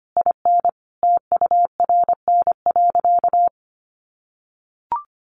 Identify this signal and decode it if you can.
Morse code, heard as a keyed tone, about 25 wpm, 700 Hz. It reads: IN TURN.
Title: Sentences from Top 700 Words 25wpm